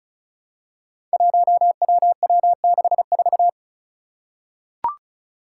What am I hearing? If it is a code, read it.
Morse code, heard as a keyed tone, about 36 wpm, 700 Hz. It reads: 1WW64
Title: Five Challenging Letter-Number Combinations 35wpm